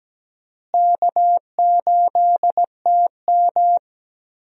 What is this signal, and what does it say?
Morse code, heard as a keyed tone, about 17 wpm, 700 Hz. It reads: K8TM
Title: USA Call Signs - 4 Characters - Repeated 2x - 17wpm